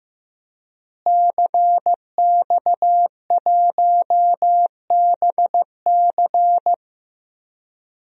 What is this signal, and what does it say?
Morse code, heard as a keyed tone, about 15 wpm, 700 Hz. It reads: CX1BC
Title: DXCC-100 Call Signs - 5 Characters - Repeated 2x - 15wpm